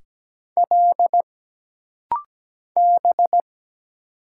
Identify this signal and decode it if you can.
Morse code, heard as a keyed tone, about 17 wpm, 700 Hz. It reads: L B
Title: Single Letter-Number Alt - Rapid-Fire 17wpm